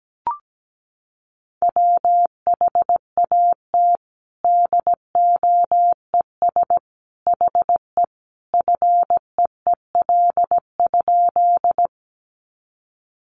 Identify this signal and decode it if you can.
Morse code, heard as a keyed tone, about 17 wpm, 700 Hz. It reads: WHAT DOES HE FEEL?
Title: Sentences from Top 400 Words 17wpm